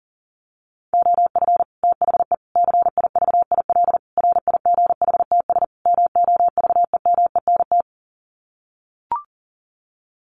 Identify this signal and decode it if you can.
Morse code, heard as a keyed tone, about 40 wpm, 700 Hz. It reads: OF THE CIVIL RIGHTS MOVEMENT
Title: Sets of 5 Words Encore2 40wpm